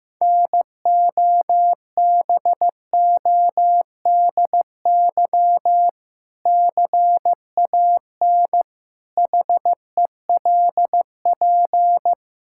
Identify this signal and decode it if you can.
Morse code, heard as a keyed tone, about 16 wpm, 700 Hz. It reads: NOBODY CAN HELP
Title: Sentences from Top 1600 Words 15wpm